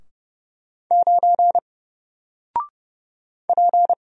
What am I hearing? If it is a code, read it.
Morse code, heard as a keyed tone, about 30 wpm, 700 Hz. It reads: 9 P
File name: Single-Letter-Number-30wpm